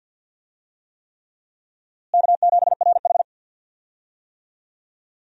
Morse code, heard as a keyed tone, about 51 wpm, 700 Hz. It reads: K7RH
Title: USA Call Signs - 4 Characters - Repeated 3x - 50wpm